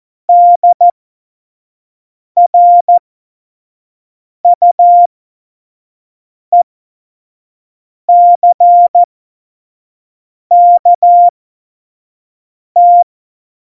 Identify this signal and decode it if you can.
Morse code, heard as a keyed tone, about 14 wpm, spaced as for 5 wpm, 700 Hz. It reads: DRUECKT
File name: Morsekurs / Witze / Witz 04 Juli 17, Tempo 25 F70